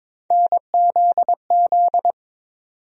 Morse code, heard as a keyed tone, about 22 wpm, 700 Hz. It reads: NZZ
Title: USA Call Sign Postfixes - 3 Letters 22wpm